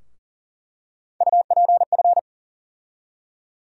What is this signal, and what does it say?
Morse code, heard as a keyed tone, about 40 wpm, 700 Hz. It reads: UPF
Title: USA Call Sign Postfixes - 3 Letters 40wpm